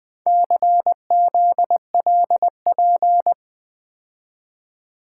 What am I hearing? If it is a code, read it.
Morse code, heard as a keyed tone, about 24 wpm, 700 Hz. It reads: CZLP
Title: Speed-Racing - Four Letter-Number Combinations - 20wpm